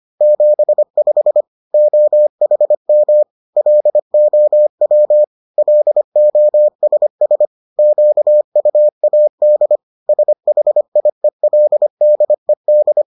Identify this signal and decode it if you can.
Morse code, heard as a keyed tone, about 25 wpm, 600 Hz. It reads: 75 OHM LOW LOSS QUAD SHIELDED